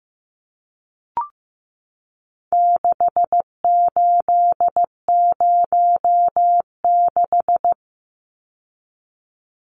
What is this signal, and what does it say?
Morse code, heard as a keyed tone, about 15 wpm, 700 Hz. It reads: 6806
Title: Numbers - 4-Digit 15wpm